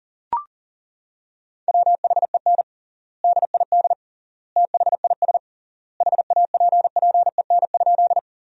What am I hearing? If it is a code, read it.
Morse code, heard as a keyed tone, about 40 wpm, 700 Hz. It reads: WHEN DID THIS HAPPEN?
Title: 3x Word-Spacing - Sentences from Top 400 Words - 40wpm